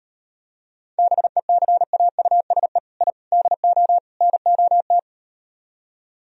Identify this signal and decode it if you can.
Morse code, heard as a keyed tone, about 38 wpm, 700 Hz. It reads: BECAUSE I DO NOT